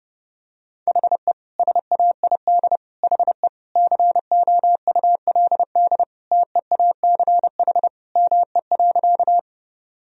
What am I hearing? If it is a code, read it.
Morse code, heard as a keyed tone, about 30 wpm, 700 Hz. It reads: HE SAID HE COULD TEACH ME.
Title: Sentences from Top 1000 Words - Review Part 2 - 30wpm